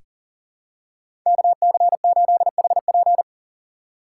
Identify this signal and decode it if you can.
Morse code, heard as a keyed tone, about 40 wpm, 700 Hz. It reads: KC8HP
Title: USA Call Signs - 5 Characters - 3x Character Spacing - 40wpm